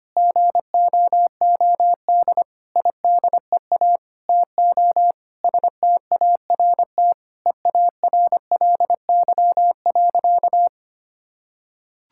Morse code, heard as a keyed tone, about 25 wpm, 700 Hz. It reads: GOOD IDEA TO START EARLY.
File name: practice / 25wpm / Sentences with 9-Word Phrases - 25wpm